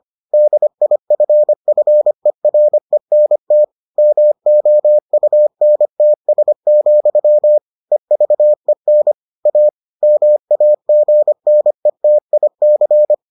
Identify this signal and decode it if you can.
Morse code, heard as a keyed tone, about 25 wpm, 600 Hz. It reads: DIFFERENT MOUNTS, EVEN A MAGNETIC